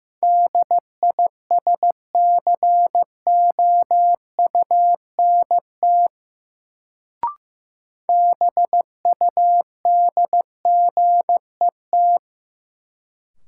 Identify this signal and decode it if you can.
Morse code, heard as a keyed tone, about 15 wpm, 700 Hz. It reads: DISCOUNT BUDGET
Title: Sentences from Top 1200 Words - New Word Drill - 15wpm